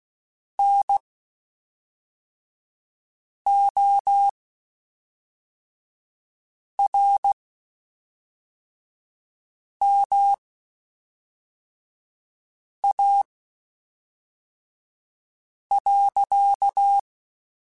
Morse code, heard as a keyed tone, about 16 wpm, spaced as for 3 wpm, 800 Hz. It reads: NORMA.
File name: QSOs / QSOs-mp3s / WD0FBT 1@5